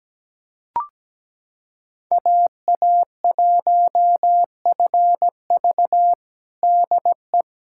Morse code, heard as a keyed tone, about 17 wpm, 700 Hz. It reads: AA1FV DE
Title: USA Callsign DE Callsign - 5 Characters - 17wpm